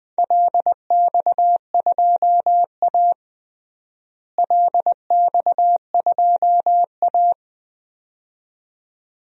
Morse code, heard as a keyed tone, about 20 wpm, 700 Hz. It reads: LX2A LX2A
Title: DXCC-100 Call Signs - 4 Characters - Repeated 2x - 20wpm